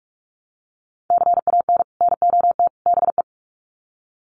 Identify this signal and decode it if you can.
Morse code, heard as a keyed tone, about 44 wpm, 700 Hz. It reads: CAN NOT BE
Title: Sets of 3 Words 45wpm